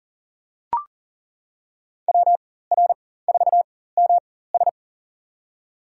Morse code, heard as a keyed tone, about 40 wpm, spaced as for 19 wpm, 700 Hz. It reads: WR4MS
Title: USA Call Signs - 5 Characters - 4x Character Spacing - 40wpm